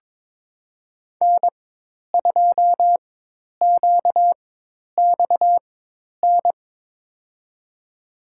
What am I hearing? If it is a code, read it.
Morse code, heard as a keyed tone, about 22 wpm, spaced as for 10 wpm, 700 Hz. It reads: N2QXN
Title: USA Call Signs - 5 Characters - 4x Character Spacing - 22wpm